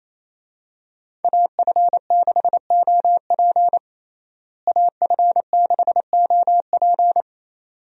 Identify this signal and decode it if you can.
Morse code, heard as a keyed tone, about 28 wpm, 700 Hz. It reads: AF6OP AF6OP
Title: USA Call Signs - 5 Characters - Repeated 2x - 28wpm